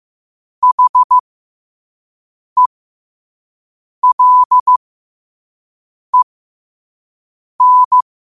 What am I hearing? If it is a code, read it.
Morse code, heard as a keyed tone, about 15 wpm, spaced as for 5 wpm, 1000 Hz. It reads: HELEN